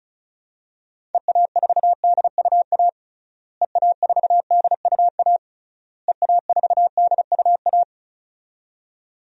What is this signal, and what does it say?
Morse code, heard as a keyed tone, about 34 wpm, 700 Hz. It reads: EA4DUA EA4DUA EA4DUA
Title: DXCC-100 Call Signs - 6 Characters - Repeated 3x - 35wpm